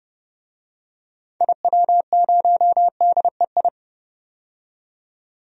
Morse code, heard as a keyed tone, about 29 wpm, 690 Hz. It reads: IW0DEI
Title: DXCC-100 Call Signs - 6 Characters - 30wpm